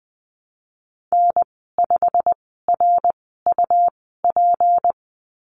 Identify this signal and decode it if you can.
Morse code, heard as a keyed tone, about 20 wpm, 700 Hz. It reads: N 5 R U P
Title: 154 Lesson 25g - Letter p - Call Signs 20wpm